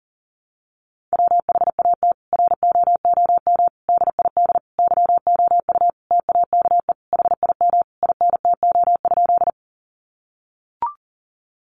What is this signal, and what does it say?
Morse code, heard as a keyed tone, about 40 wpm, 700 Hz. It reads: WHAT ROOM DID YOU TAKE HIM INTO?
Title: Sentences from Top 600 Words 40wpm